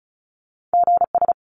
Morse code, heard as a keyed tone, about 36 wpm, 700 Hz. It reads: GS
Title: Top Two Letter Combinations in Words 35wpm